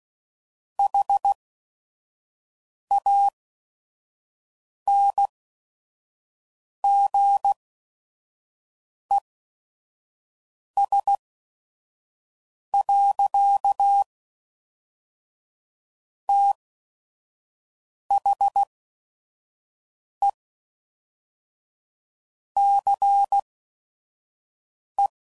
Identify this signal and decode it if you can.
Morse code, heard as a keyed tone, about 16 wpm, 800 Hz. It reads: H A N G E S . T H E C E